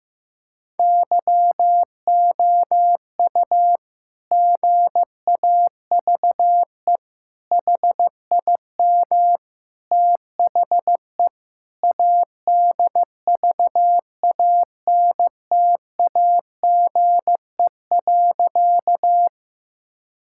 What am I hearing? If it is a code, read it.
Morse code, heard as a keyed tone, about 15 wpm, 700 Hz. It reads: YOU GAVE HIM THE ADVANTAGE.
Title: Sentences from Top 1000 Words - Review Part 6 - 15wpm